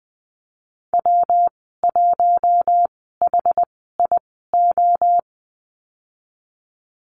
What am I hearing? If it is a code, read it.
Morse code, heard as a keyed tone, about 20 wpm, spaced as for 15 wpm, 700 Hz. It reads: W1HIO
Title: 147 Lesson 24g - Letter y - Call Signs 20wpm